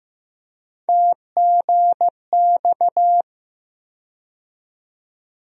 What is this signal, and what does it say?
Morse code, heard as a keyed tone, about 15 wpm, 700 Hz. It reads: TGX
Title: USA Call Sign Postfixes - 3 Letters 15wpm